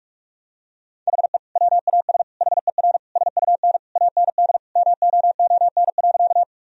Morse code, heard as a keyed tone, about 46 wpm, 700 Hz. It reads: HE WAS HER SUN AND MOON.
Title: Sentences from Top 1000 Words - Review Part 3 - 45wpm